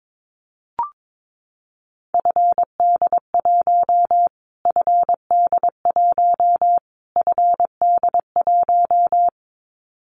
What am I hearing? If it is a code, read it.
Morse code, heard as a keyed tone, about 22 wpm, 700 Hz. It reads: FD1 FD1 FD1
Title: Callsign Prefixes for Top 25 DXCC - Repeated 3x - 22wpm